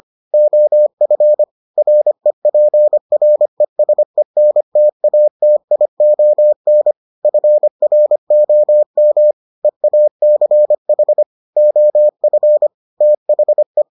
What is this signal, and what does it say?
Morse code, heard as a keyed tone, about 25 wpm, 600 Hz. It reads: OF REPRESENTATION FROM EACH OF THE